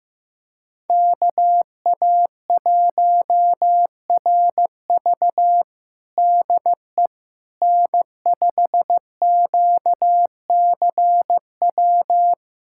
Morse code, heard as a keyed tone, about 15 wpm, 700 Hz. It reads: KA1RV DE N5QCW
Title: USA Callsign DE Callsign - 5 Characters - 15wpm